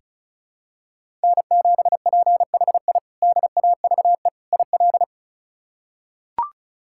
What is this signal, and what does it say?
Morse code, heard as a keyed tone, about 34 wpm, 700 Hz. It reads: N7PHI DAVE IL
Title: K1USN SST QSO Practice - 35wpm